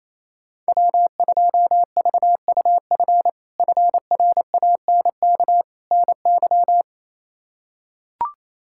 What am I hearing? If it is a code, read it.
Morse code, heard as a keyed tone, about 27 wpm, 700 Hz. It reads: W2VUF FRANK NY
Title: K1USN SST QSO Practice - 28wpm